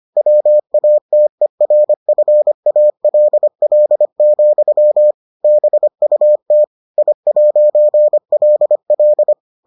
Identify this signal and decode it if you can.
Morse code, heard as a keyed tone, about 25 wpm, 600 Hz. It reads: WATERFALL, BUT I'LL